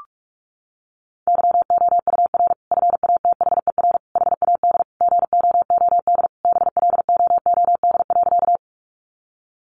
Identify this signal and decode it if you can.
Morse code, heard as a keyed tone, about 46 wpm, 700 Hz. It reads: YOUR FATHER HAD GOOD BLOOD.
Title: Sentences from Top 1100 Words 45wpm